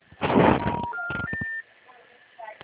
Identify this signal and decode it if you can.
Special information tone, with noise.
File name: out- -2009- - - .81431